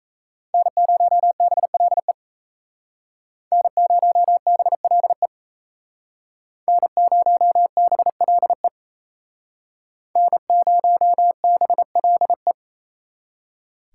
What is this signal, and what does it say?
Morse code, heard as a keyed tone, about 37 wpm, 700 Hz. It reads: N0BLE N0BLE N0BLE N0BLE